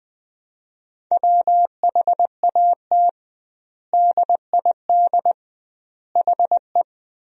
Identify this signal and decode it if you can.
Morse code, heard as a keyed tone, about 20 wpm, 700 Hz. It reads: WHAT DID HE
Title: 2x Word-Spacing - Sentences from Top 300 Words - 20wpm